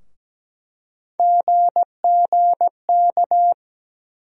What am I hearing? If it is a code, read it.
Morse code, heard as a keyed tone, about 17 wpm, 700 Hz. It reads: GGK
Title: USA Call Sign Postfixes - 3 Letters 17wpm